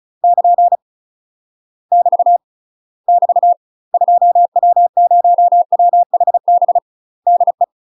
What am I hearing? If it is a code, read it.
Morse code, heard as a keyed tone, about 35 wpm, 700 Hz. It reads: <KN> <BT> <BT> 2W0WHB DE
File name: UK MASTER QSOs TRAINING / 35WPM / UK MASTER QSOs 0005